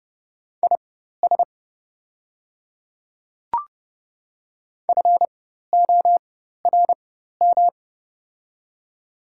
Morse code, heard as a keyed tone, about 32 wpm, spaced as for 14 wpm, 700 Hz. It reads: IS FORM
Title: 4x Character Spacing - Top 200 Words - 30wpm